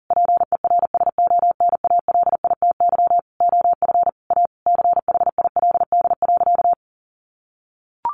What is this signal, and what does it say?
Morse code, heard as a keyed tone, about 40 wpm, 700 Hz. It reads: PERSONALITY OF A CHILD.
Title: Sentences from Top 1400 Words 40wpm